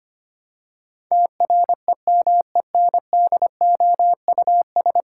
Morse code, heard as a keyed tone, about 25 wpm, 700 Hz. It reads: TREMENDOUS
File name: Ten Letter Words - 25wpm